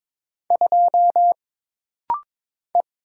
Morse code, heard as a keyed tone, about 22 wpm, 700 Hz. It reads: 2 E
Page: This rapid-fire timing is geared to push your ICR (Instant Character Recognition) skills.